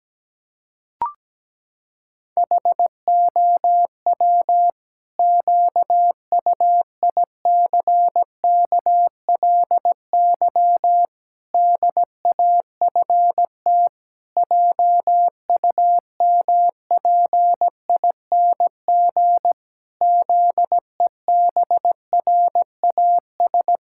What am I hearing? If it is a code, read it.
Morse code, heard as a keyed tone, about 17 wpm, 700 Hz. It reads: HOW QUICKLY DAFT JUMPING ZEBRAS
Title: Pangram Challenge v2 - 17wpm